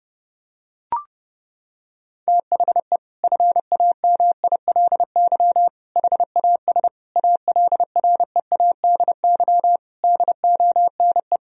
Morse code, heard as a keyed tone, about 30 wpm, 700 Hz. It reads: THE FAMILY HAS ALREADY DONE
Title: Sentences from Top 1000 Words - Review Part 2 - 30wpm